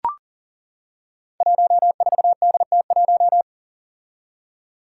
Morse code, heard as a keyed tone, about 40 wpm, 700 Hz. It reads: 14DT1
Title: Five Challenging Letter-Number Combinations 40wpm